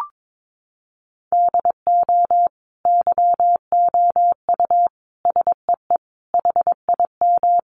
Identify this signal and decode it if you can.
Morse code, heard as a keyed tone, about 22 wpm, 700 Hz. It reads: DO YOU SEE HIM
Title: Sentences from Top 1400 Words 22wpm